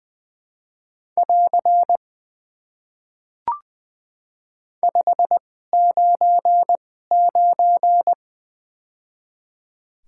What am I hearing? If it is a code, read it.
Morse code, heard as a keyed tone, about 20 wpm, spaced as for 15 wpm, 700 Hz. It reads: <AR> 599
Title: Top 90 QSO Elements - 2x Character Spacing - 20wpm